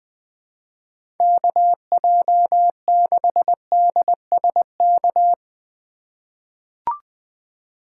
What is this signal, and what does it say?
Morse code, heard as a keyed tone, about 20 wpm, 700 Hz. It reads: KJ6DSK
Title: USA Call Signs - 6 Characters - 3x Character Spacing - 20wpm